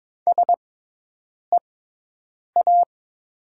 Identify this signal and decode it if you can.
Morse code, heard as a keyed tone, about 22 wpm, spaced as for 8 wpm, 700 Hz. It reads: SEA
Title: 6x Character Spacing - Top 500 Words - 22wpm